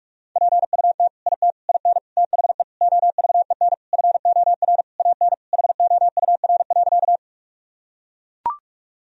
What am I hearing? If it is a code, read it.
Morse code, heard as a keyed tone, about 42 wpm, 700 Hz. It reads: PUT IT IN THE OVEN FOR AN HOUR.
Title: Sentences from Top 1000 Words - Review Part 4 - 45wpm